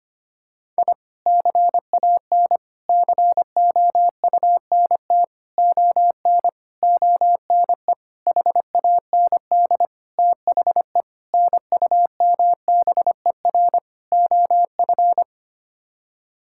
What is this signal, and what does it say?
Morse code, heard as a keyed tone, about 25 wpm, 700 Hz. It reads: I CAN COUNT ON ONE HAND THE NUMBER OF
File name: Sets of 9 Words - 25wpm